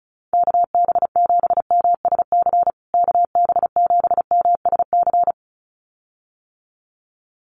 Morse code, heard as a keyed tone, about 34 wpm, 700 Hz. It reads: KB7MSC KB7MSC
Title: USA Call Signs Repeated 2x 35wpm